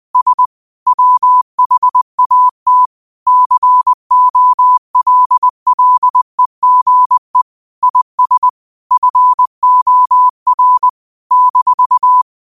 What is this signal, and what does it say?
Morse code, heard as a keyed tone, about 20 wpm, 1000 Hz. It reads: S WHAT COLLEGE IS FOR -